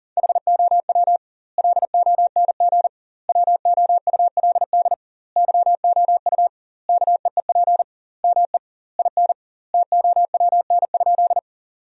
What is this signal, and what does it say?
Morse code, heard as a keyed tone, about 40 wpm, 700 Hz. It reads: HOW LONG WOULD YOU KEEP ME IN TOWN?
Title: 2x Word-Spacing - Sentences from Top 500 Words - 40wpm